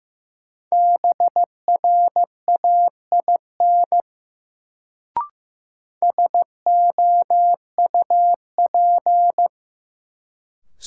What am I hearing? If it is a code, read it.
Morse code, heard as a keyed tone, about 15 wpm, 700 Hz. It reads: BRAIN SOUP
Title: Sentences from Top 1000 Words - Review Part 6 - New Word Drill - 15wpm